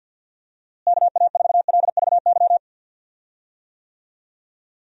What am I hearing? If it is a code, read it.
Morse code, heard as a keyed tone, about 47 wpm, 700 Hz. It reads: KA4LVY
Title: USA Call Signs - 6 Characters - Repeated 2x - 50wpm